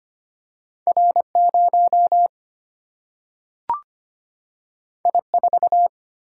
Morse code, heard as a keyed tone, about 25 wpm, 700 Hz. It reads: R0 I4
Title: Single-Letter Single-Number 25wpm